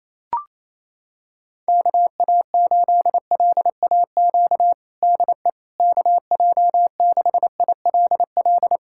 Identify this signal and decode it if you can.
Morse code, heard as a keyed tone, about 28 wpm, 700 Hz. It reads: KA8LAQ DE KJ6ILL
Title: USA Callsign DE Callsign 28wpm